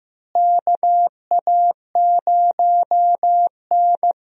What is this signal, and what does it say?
Morse code, heard as a keyed tone, about 15 wpm, 700 Hz. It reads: KA0N
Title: USA Call Signs - 4 Characters - Repeated 3x - 15wpm